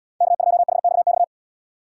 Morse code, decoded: K2HXX

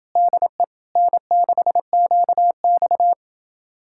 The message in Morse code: DE N6QX

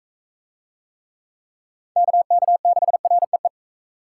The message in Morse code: KK6REE

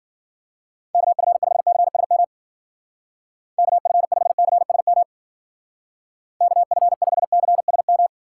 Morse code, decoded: KF5CSM KF5CSM KF5CSM

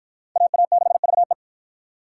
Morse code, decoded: AA6VE

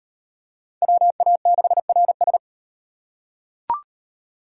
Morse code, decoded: WA6RS

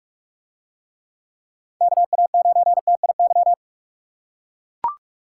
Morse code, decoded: KA9TIY

